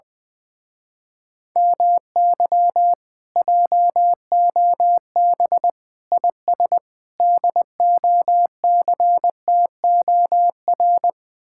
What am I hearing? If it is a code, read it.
Morse code, heard as a keyed tone, about 20 wpm, 700 Hz. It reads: MY JOB IS DOCTOR
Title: Speed-Racing Common QSO Phrases - 20wpm